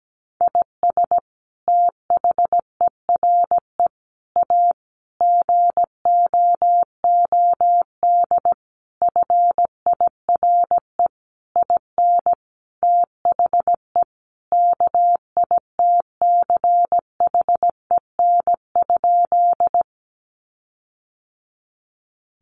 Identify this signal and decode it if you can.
Morse code, heard as a keyed tone, about 17 wpm, 700 Hz. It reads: IS THERE A GOOD FIRE IN THE KITCHEN?
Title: Sentences from Top 1400 Words 17wpm